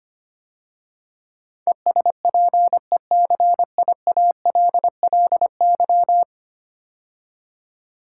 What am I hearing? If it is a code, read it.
Morse code, heard as a keyed tone, about 25 wpm, 700 Hz. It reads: ESPECIALLY
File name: 10 Letter Words - First 3 Letters Given - 25wpm